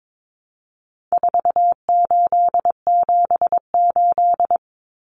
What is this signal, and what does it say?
Morse code, heard as a keyed tone, about 22 wpm, 700 Hz. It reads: 4878
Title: Common Cut Numbers - 4-Digit 22wpm